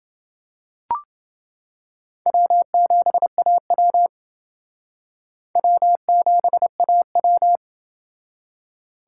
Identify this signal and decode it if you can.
Morse code, heard as a keyed tone, about 28 wpm, 700 Hz. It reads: W7AW W7AW